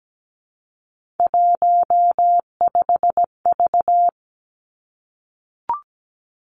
Morse code, heard as a keyed tone, about 17 wpm, 700 Hz. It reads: 15V
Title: Three Challenging Letter-Number Combinations 17wpm